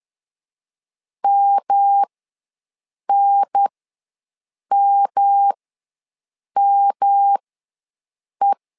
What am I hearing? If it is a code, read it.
Morse code, heard as a keyed tone, about 10 wpm, spaced as for 6 wpm, 800 Hz. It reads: MNMME
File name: LEZIONI / 3 MWRO / 3 M ws 10wpm 2121